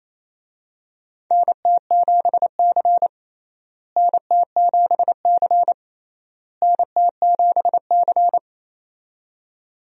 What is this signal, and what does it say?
Morse code, heard as a keyed tone, about 28 wpm, 700 Hz. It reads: NT7C NT7C NT7C
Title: USA Call Signs - 4 Characters - Repeated 3x - 28wpm